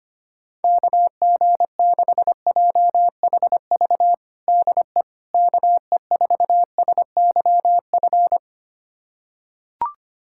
Morse code, heard as a keyed tone, about 25 wpm, 700 Hz. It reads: KG6JHV DE KE4SYF
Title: USA Callsign DE Callsign - 6 Characters - 25wpm